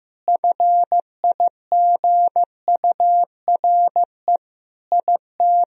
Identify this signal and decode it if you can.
Morse code, heard as a keyed tone, about 16 wpm, 700 Hz. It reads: FIGURE IT
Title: Sentences from Top 1000 Words - Review Part 2 - 15wpm